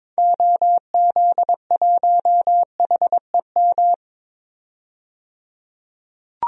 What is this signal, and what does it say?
Morse code, heard as a keyed tone, about 22 wpm, 700 Hz. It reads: OZ1HEM
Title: DXCC-100 Call Signs - 6 Characters - 22wpm